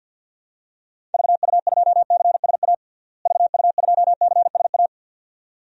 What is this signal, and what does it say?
Morse code, heard as a keyed tone, about 53 wpm, 700 Hz. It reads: VU2CSA VU2CSA
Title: DXCC-100 Call Signs - 6 Characters - Repeated 2x - 50wpm